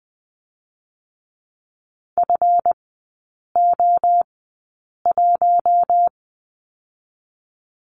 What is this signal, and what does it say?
Morse code, heard as a keyed tone, about 21 wpm, spaced as for 8 wpm, 690 Hz. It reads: FO1
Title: 159 Lesson 26e - Letter g - Three Characters 20wpm